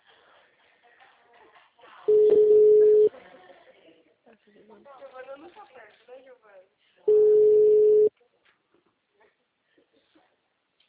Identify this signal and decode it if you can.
Ringback tone (Europe, 425 Hz), with speech.